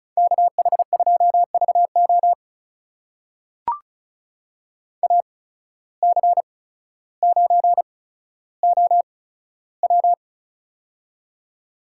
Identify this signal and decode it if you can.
Morse code, heard as a keyed tone, about 34 wpm, spaced as for 15 wpm, 700 Hz. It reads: KH2VO AC9OW
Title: USA Call Signs - 5 Characters - 8x Character Spacing - 35wpm